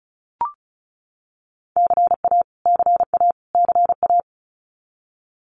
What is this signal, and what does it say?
Morse code, heard as a keyed tone, about 34 wpm, 700 Hz. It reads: CA CA CA
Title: US States Repeated 3x 35wpm